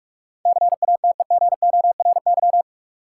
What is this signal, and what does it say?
Morse code, heard as a keyed tone, about 44 wpm, 700 Hz. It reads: CATEGORY
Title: Sentences from Top 1000 Words - Review Part 7 - New Word Drill - 45wpm